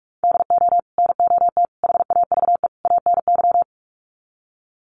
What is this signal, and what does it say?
Morse code, heard as a keyed tone, about 44 wpm, 700 Hz. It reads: DO NOT HAVE ANY
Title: Sets of 4 Words 45wpm